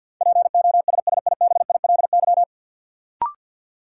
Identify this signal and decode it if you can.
Morse code, heard as a keyed tone, about 53 wpm, 700 Hz. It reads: POSSIBILY
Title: Sentences from Top 1300 Words - New Word Drill Alt - 50wpm